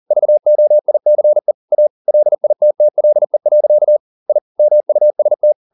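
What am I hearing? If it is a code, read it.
Morse code, heard as a keyed tone, about 40 wpm, 600 Hz. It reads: VOICE A LITTLE. I MUST
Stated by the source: Morse code at 40 wpm, practice text, CW med Gnister.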